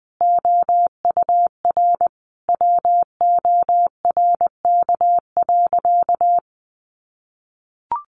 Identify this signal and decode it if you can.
Morse code, heard as a keyed tone, about 20 wpm, 700 Hz. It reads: OUR WORK.
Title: Sentences from Top 1300 Words 20wpm